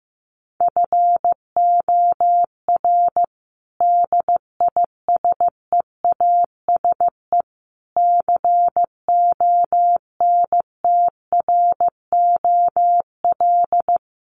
Morse code, heard as a keyed tone, about 15 wpm, 700 Hz. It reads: FOR DISEASE CONTROL
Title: Sets of 5 Words Encore2 15wpm